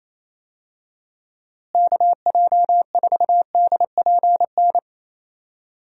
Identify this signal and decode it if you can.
Morse code, heard as a keyed tone, about 28 wpm, 700 Hz. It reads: KJ4DPN
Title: USA Call Signs - 6 Characters - Repeated 3x - 28wpm